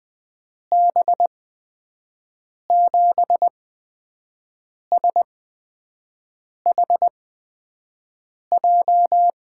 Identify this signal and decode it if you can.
Morse code, heard as a keyed tone, about 20 wpm, spaced as for 5 wpm, 700 Hz. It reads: B7SHJ
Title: USA Call Signs - 8x Character Spacing - 20wpm